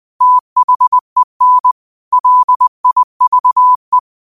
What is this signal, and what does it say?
Morse code, heard as a keyed tone, about 20 wpm, 1000 Hz. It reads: THEN LIVE
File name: qotdfiles / 20wpm